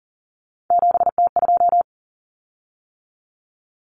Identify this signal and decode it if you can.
Morse code, heard as a keyed tone, about 40 wpm, 700 Hz. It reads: 7T2
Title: Common Cut Numbers - 3-Digit 40wpm